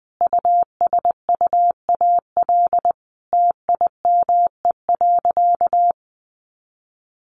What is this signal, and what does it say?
Morse code, heard as a keyed tone, about 20 wpm, 700 Hz. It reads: USUAL TIME.
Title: Sentences from Top 400 Words 20wpm